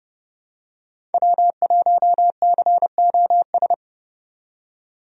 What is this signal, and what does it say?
Morse code, heard as a keyed tone, about 30 wpm, 700 Hz. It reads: W1COS
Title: USA Call Signs - 5 Characters - 30wpm